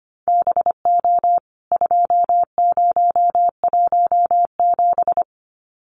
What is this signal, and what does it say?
Morse code, heard as a keyed tone, about 25 wpm, 700 Hz. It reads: BO 2017
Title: CWT Exchange 25wpm